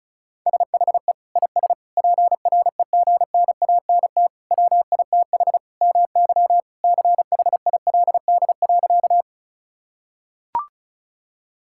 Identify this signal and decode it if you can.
Morse code, heard as a keyed tone, about 36 wpm, 700 Hz. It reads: SHE IS PREGNANT WITH MY CHILD.
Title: Sentences from Top 1000 Words - Review Part 6 - 35wpm